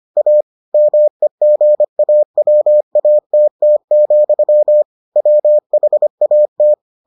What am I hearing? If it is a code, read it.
Morse code, heard as a keyed tone, about 25 wpm, 600 Hz. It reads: A MEGAWATT, WHAT